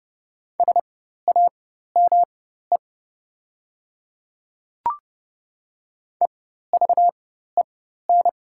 Morse code, heard as a keyed tone, about 30 wpm, spaced as for 14 wpm, 700 Hz. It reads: SAME EVEN
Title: 4x Character Spacing - Top 200 Words - 30wpm